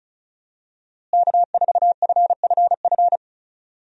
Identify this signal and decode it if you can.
Morse code, heard as a keyed tone, about 34 wpm, 700 Hz. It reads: K4FFF
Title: USA Call Signs - 5 Characters - Repeated 3x - 35wpm